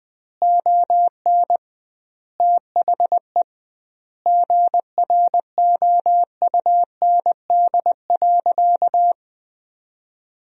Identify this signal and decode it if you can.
Morse code, heard as a keyed tone, about 20 wpm, 700 Hz. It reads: ON THE GROUND.
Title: 2x Word-Spacing - Sentences from Top 500 Words - 20wpm